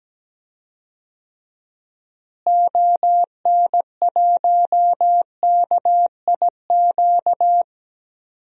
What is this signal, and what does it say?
Morse code, heard as a keyed tone, about 17 wpm, 700 Hz. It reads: ON1KIQ
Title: DXCC-100 Call Signs - 6 Characters - Repeated 2x - 17wpm